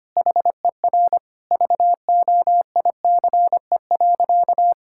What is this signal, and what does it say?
Morse code, heard as a keyed tone, about 25 wpm, 700 Hz. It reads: HER VOICE.
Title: Sentences from Top 1600 Words 25wpm